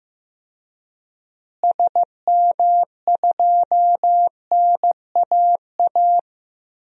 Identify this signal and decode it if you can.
Morse code, heard as a keyed tone, about 15 wpm, 700 Hz. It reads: SM2NAA
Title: DXCC-100 Call Signs - 6 Characters - Repeated 3x - 15wpm